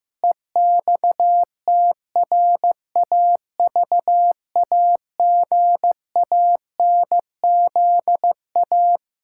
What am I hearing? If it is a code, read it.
Morse code, heard as a keyed tone, about 15 wpm, 700 Hz. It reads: EXTRAVAGANZA
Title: Pangram Challenge v2 - 15wpm